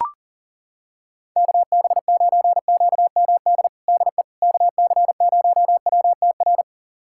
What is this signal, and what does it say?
Morse code, heard as a keyed tone, about 37 wpm, 700 Hz. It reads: KB9QMD DE KC0WTR
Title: USA Callsign DE Callsign 40wpm